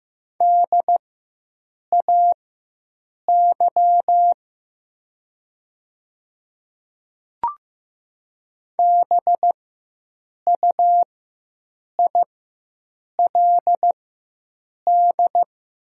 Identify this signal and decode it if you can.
Morse code, heard as a keyed tone, about 15 wpm, spaced as for 7 wpm, 690 Hz. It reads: DAY BUILD
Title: 4x Character Spacing - Top 200 Words - 15wpm